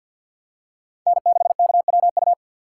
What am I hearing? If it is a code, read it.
Morse code, heard as a keyed tone, about 50 wpm, 700 Hz. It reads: N6KWU
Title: USA Call Signs - 5 Characters - 6x Character Spacing - 50wpm